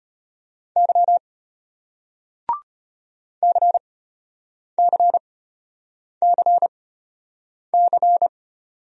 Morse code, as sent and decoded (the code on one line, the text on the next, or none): -.-- / -.-. -.-. -.-. -.-.
Y CCCC